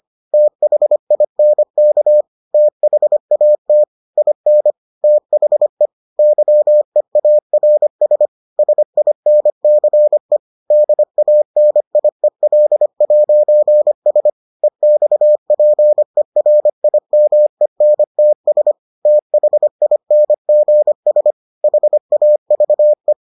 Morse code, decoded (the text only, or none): THINK THAT IN THE YEARS SINCE DANIEL'S EXPERIMENTS THINGS HAVE